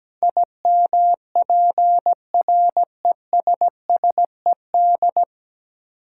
Morse code, heard as a keyed tone, about 17 wpm, 700 Hz. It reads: IMPRESSED
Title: Sentences from Top 1600 Words - New Word Drill - 17wpm